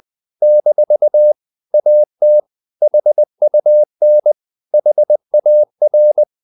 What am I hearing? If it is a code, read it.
Morse code, heard as a keyed tone, about 20 wpm, 600 Hz. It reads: - AT HUN HAR
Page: Ferie 20 wpm